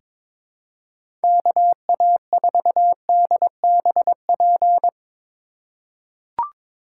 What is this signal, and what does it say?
Morse code, heard as a keyed tone, about 22 wpm, 700 Hz. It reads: KA4DBP
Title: USA Call Signs - 6 Characters - 4x Character Spacing - 22wpm